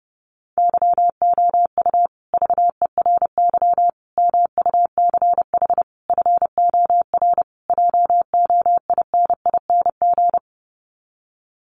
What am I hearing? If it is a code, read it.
Morse code, heard as a keyed tone, about 30 wpm, 700 Hz. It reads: YOU VERY MUCH FOR JOINING
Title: Sets of 5 Words Encore 30wpm